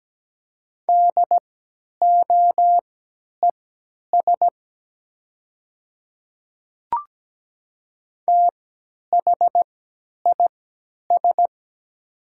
Morse code, heard as a keyed tone, about 17 wpm, spaced as for 10 wpm, 700 Hz. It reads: DOES THIS
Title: 3x Character Spacing - Top 500 Words - 17wpm